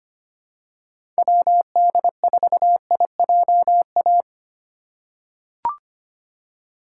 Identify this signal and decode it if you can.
Morse code, heard as a keyed tone, about 25 wpm, 700 Hz. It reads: WD4IJA
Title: USA Call Signs - 6 Characters - 2x Character Spacing - 25wpm